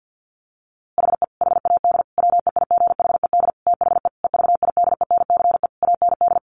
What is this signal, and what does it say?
Morse code, heard as a keyed tone, about 50 wpm, 700 Hz. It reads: HE HAD WEIGHED THE EVIDENCE AND
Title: A Sound Of Thunder 50wpm